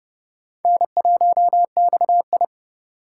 Morse code, heard as a keyed tone, about 40 wpm, 700 Hz. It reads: N1XI